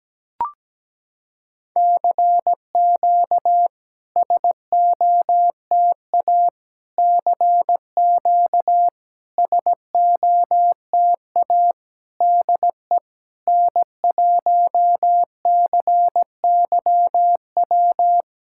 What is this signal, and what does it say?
Morse code, heard as a keyed tone, about 17 wpm, 700 Hz. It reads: CQ SOTA CQ SOTA DE N1CYW
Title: SOTA QSO Practice 17wpm